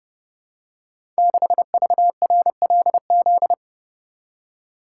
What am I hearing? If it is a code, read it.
Morse code, heard as a keyed tone, about 30 wpm, 700 Hz. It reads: 6VRLZ
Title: Code Groups - 5 Letter:Numbers - 30wpm